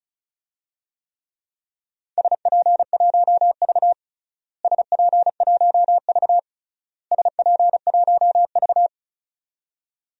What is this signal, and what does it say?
Morse code, heard as a keyed tone, about 36 wpm, 700 Hz. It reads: SP1V SP1V SP1V
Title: DXCC-100 Call Signs - 4 Characters - Repeated 3x - 35wpm